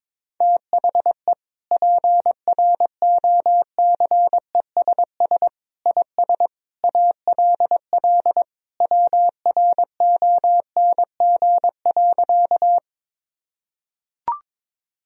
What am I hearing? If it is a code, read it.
Morse code, heard as a keyed tone, about 22 wpm, 700 Hz. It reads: THE PROCESS IS ALL WRONG.
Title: Sentences from Top 1000 Words - Review Part 3 - 22wpm